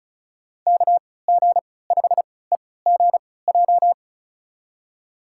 Morse code, heard as a keyed tone, about 34 wpm, spaced as for 20 wpm, 700 Hz. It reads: KG5EGJ
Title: USA Call Signs - 3x Character Spacing - 35wpm